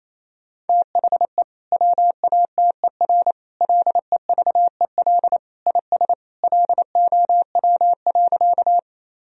Morse code, heard as a keyed tone, about 28 wpm, 700 Hz. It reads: THE WATER LEVEL IS LOW.
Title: Sentences from Top 1000 Words - Review Part 4 - 28wpm